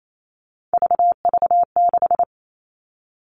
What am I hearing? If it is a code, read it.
Morse code, heard as a keyed tone, about 28 wpm, 700 Hz. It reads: VV6
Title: All Cut Numbers - 3-Digit 28wpm